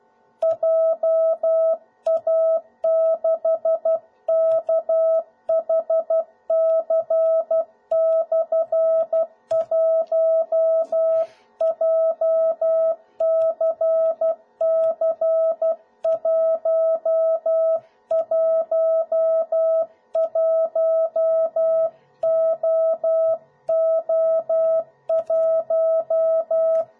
Morse code, heard as a keyed tone, about 12 wpm, 650 Hz. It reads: JA6KHC/1 JCC111OO1
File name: jh8cbh / 2021cw / cw01b